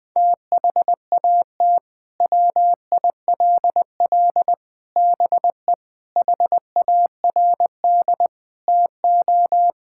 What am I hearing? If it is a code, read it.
Morse code, heard as a keyed tone, about 20 wpm, 700 Hz. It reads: THAT WILL BE HARD TO
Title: Sentences from Top 1000 Words - Review Part 5 - 20wpm